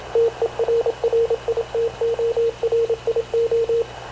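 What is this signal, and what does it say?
Morse code, heard as a keyed tone, about 27 wpm, 460 Hz, with radio noise. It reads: TERRITORIO
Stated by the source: Havana Radio / CLA sending Spanish press (PX) in Morse, recorded by Radio Officer Paul Dumesnil onboard CNAV QUEST/CZDO.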